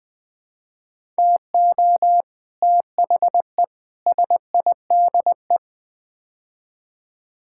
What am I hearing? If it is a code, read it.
Morse code, heard as a keyed tone, about 21 wpm, 700 Hz. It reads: TO THE SIDE
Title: Build A Phrase - 5-Word Phrases - 20wpm